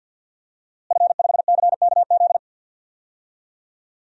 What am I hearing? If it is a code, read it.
Morse code, heard as a keyed tone, about 47 wpm, 700 Hz. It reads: F5CKZ
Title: DXCC-100 Call Signs - 5 Characters - Repeated 2x - 50wpm